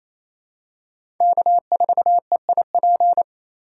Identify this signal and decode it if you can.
Morse code, heard as a keyed tone, about 28 wpm, 700 Hz. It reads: K4EIP
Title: USA Call Signs - 5 Characters - 4x Character Spacing - 28wpm